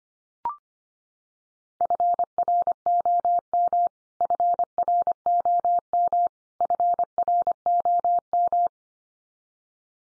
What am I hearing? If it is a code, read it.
Morse code, heard as a keyed tone, about 25 wpm, 700 Hz. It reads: FROM FROM FROM
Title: Top 100 Words 25wpm x3